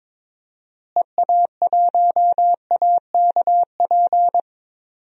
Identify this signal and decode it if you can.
Morse code, heard as a keyed tone, about 22 wpm, 700 Hz. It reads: EA1AKP
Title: DXCC-100 Call Signs - 6 Characters - Repeated 3x - 22wpm